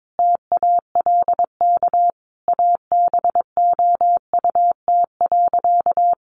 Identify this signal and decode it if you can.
Morse code, heard as a keyed tone, about 22 wpm, 700 Hz. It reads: TALK ABOUT.
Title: Sentences from Top 800 Words 22wpm